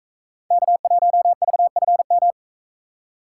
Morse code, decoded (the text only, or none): K1VFM